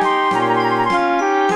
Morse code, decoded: none